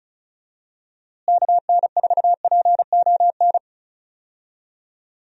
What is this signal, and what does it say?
Morse code, heard as a keyed tone, about 36 wpm, 700 Hz. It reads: KN4PON
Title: USA Call Signs - 6 Characters - Repeated 3x - 35wpm